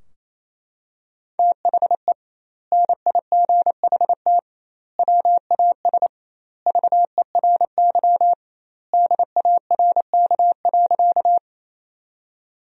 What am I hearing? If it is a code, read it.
Morse code, heard as a keyed tone, about 28 wpm, 700 Hz. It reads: THE NIGHT WAS VERY DARK.
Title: 2x Word-Spacing - Sentences from Top 500 Words - 28wpm